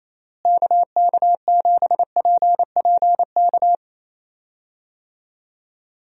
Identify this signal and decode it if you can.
Morse code, heard as a keyed tone, about 28 wpm, 700 Hz. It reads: KK7PPK
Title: USA Call Signs - 6 Characters - Repeated 2x - 28wpm